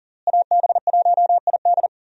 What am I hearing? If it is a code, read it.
Morse code, heard as a keyed tone, about 37 wpm, 700 Hz. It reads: AB1ID
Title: USA Call Signs - 5 Characters - Repeated 3x - 40wpm